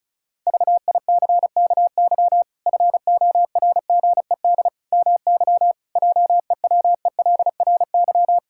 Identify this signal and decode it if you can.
Morse code, heard as a keyed tone, about 34 wpm, 700 Hz. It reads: VICKY FORGED MY JEWELRY
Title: Pangram Challenge v2 - 35wpm